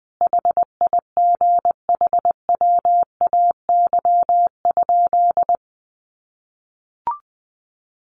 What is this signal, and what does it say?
Morse code, heard as a keyed tone, about 20 wpm, 700 Hz. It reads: HIGHWAY?
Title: Sentences from Top 1400 Words 20wpm